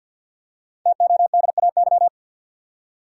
Morse code, heard as a keyed tone, about 47 wpm, 700 Hz. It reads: TODAY
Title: Five Letter Words - 50wpm